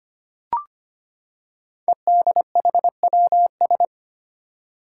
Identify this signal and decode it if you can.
Morse code, heard as a keyed tone, about 25 wpm, 700 Hz. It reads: EDHWS
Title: Five Challenging Letter-Number Combinations 25wpm